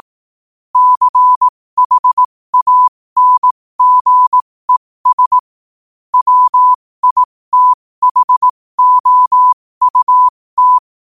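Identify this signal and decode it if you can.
Morse code, heard as a keyed tone, about 18 wpm, spaced as for 15 wpm, 1000 Hz. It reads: CHANGES WITHOUT
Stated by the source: Morse code quote of the day sent at 15 words per minute.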